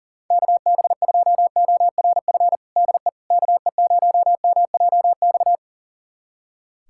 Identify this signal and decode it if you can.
Morse code, heard as a keyed tone, about 40 wpm, 700 Hz. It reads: KB2ORF DE KE0MJX
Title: USA Callsign DE Callsign - 6 Characters - 40wpm